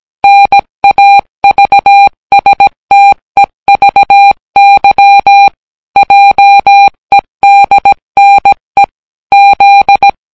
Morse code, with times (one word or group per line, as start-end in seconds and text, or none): 0.2-5.5 NAVSTEVY
6.0-8.8 JEDNE
9.3-10.1 Z